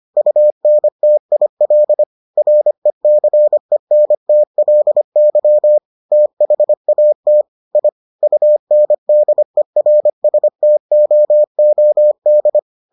Morse code, heard as a keyed tone, about 25 wpm, 600 Hz. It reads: UNTIL RECENTLY THAT I UNDERSTOOD